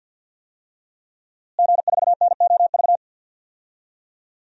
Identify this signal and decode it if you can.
Morse code, heard as a keyed tone, about 53 wpm, 700 Hz. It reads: G4NOV
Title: DXCC-100 Call Signs - 5 Characters - Repeated 2x - 50wpm